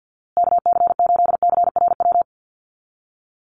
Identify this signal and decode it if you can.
Morse code, heard as a keyed tone, about 48 wpm, 700 Hz. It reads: KC8CRW
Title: USA Call Signs Repeated 2x 50wpm